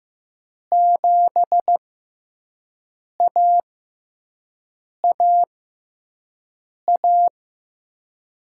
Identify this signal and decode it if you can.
Morse code, heard as a keyed tone, about 15 wpm, spaced as for 5 wpm, 700 Hz. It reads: 7AAA
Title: USA Call Signs - 6 Characters - 6x Character Spacing - 15wpm